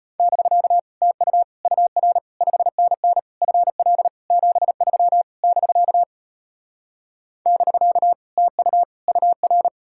Morse code, heard as a keyed tone, about 33 wpm, 700 Hz. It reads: <BK> TU UR 5NN FL 73 <BK> <BK> TU UR